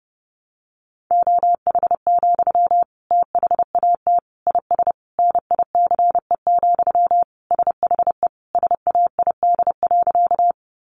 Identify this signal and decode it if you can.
Morse code, heard as a keyed tone, about 30 wpm, 700 Hz. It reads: OH, THAT IS NICE, SHE SAID.
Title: Sentences from Top 1100 Words 30wpm